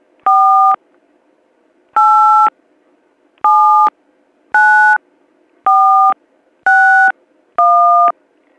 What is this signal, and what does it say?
DTMF, heard as telephone touch tones, tones of about 500 ms. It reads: 4 8 79461